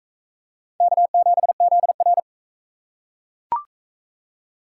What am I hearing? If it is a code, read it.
Morse code, heard as a keyed tone, about 40 wpm, 700 Hz. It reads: K7ZR